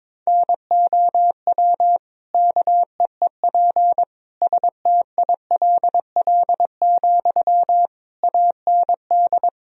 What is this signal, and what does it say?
Morse code, heard as a keyed tone, about 22 wpm, 700 Hz. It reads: NOW KEEP STILL, AND
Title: Sentences from Top 1000 Words - Review Part 2 - 22wpm